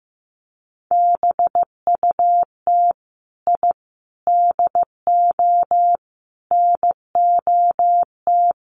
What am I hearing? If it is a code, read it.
Morse code, heard as a keyed tone, about 15 wpm, 700 Hz. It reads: BUT I DO NOT
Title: Sets of 5 Words 15wpm